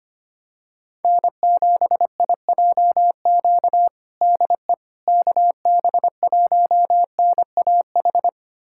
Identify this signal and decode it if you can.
Morse code, heard as a keyed tone, about 25 wpm, 700 Hz. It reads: N7IJQ DE KB1NAH
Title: USA Callsign DE Callsign 25wpm